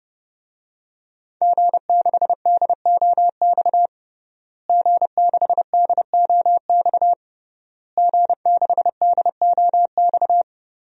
Morse code, heard as a keyed tone, about 30 wpm, 700 Hz. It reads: G6DOX G6DOX G6DOX
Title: DXCC-100 Call Signs - 5 Characters - Repeated 3x - 30wpm